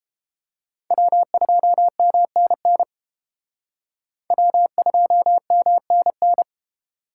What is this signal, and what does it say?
Morse code, heard as a keyed tone, about 29 wpm, 700 Hz. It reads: W2MNN W2MNN